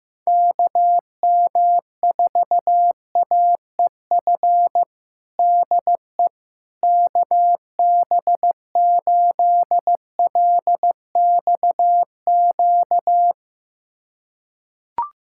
Morse code, heard as a keyed tone, about 15 wpm, 700 Hz. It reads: KM4AEF DE KB8LXQ
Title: USA Callsign DE Callsign - 6 Characters - 15wpm